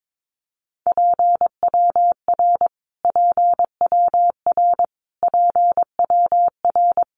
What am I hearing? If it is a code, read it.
Morse code, heard as a keyed tone, about 22 wpm, 700 Hz. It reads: PWR PWR PWR
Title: Abbreviations Repeated 3x - 22wpm